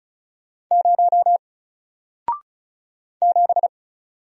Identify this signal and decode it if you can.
Morse code, heard as a keyed tone, about 37 wpm, 700 Hz. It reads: 0 7
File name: Single-Letter-Number-35wpm